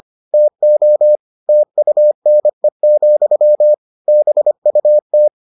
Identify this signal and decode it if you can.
Morse code, heard as a keyed tone, about 25 wpm, 600 Hz. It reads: TO TUNE, BUT